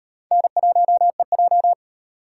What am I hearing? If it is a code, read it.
Morse code, heard as a keyed tone, about 35 wpm, 700 Hz. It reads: N1EJ